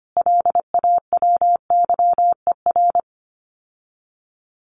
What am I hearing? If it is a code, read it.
Morse code, heard as a keyed tone, about 25 wpm, 700 Hz. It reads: LAWYER
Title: Common QSO Phrases Encore 25wpm